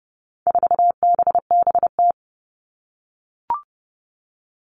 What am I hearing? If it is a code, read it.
Morse code, heard as a keyed tone, about 30 wpm, 700 Hz. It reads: 4BBT
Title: All Cut Numbers - 4-Digit 30wpm